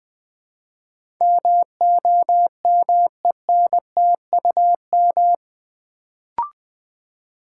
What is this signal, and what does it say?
Morse code, heard as a keyed tone, about 20 wpm, 700 Hz. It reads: MOMENTUM
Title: Build A Word - 3 Letters At A Time - 8-Letter Words - 20wpm